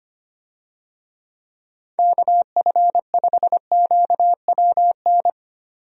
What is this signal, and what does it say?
Morse code, heard as a keyed tone, about 25 wpm, 700 Hz. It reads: KF5QWN
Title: USA Call Signs - 6 Characters - 25wpm